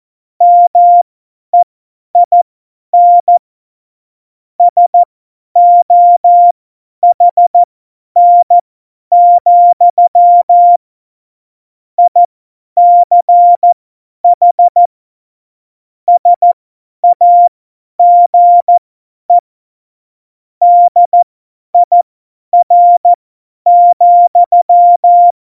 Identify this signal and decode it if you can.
Morse code, heard as a keyed tone, about 14 wpm, spaced as for 10 wpm, 700 Hz. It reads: MEIN SOHN, ICH SAGE DIR,
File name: Morsekurs / Witze / Witz 31. Oktober 17, Tempo 50